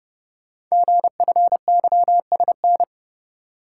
Morse code, heard as a keyed tone, about 28 wpm, 700 Hz. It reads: GFYSN
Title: Code Groups - 5 Letter:Numbers - 30wpm